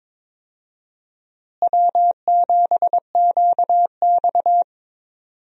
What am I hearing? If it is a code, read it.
Morse code, heard as a keyed tone, about 21 wpm, 700 Hz. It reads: W7QX
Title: USA Call Signs - 4 Characters - Repeated 2x - 22wpm